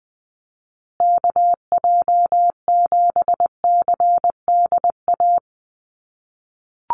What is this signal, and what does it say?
Morse code, heard as a keyed tone, about 20 wpm, 700 Hz. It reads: KJ7CDA
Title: USA Call Signs Repeated 2x 20wpm